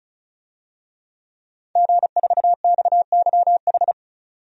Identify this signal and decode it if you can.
Morse code, heard as a keyed tone, about 36 wpm, 700 Hz. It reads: G4XYH
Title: DXCC-100 Call Signs - 5 Characters - 35wpm